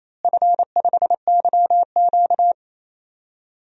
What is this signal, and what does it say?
Morse code, heard as a keyed tone, about 28 wpm, 700 Hz. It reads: F5YQ
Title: DXCC-100 Call Signs - 4 Characters - Repeated 3x - 28wpm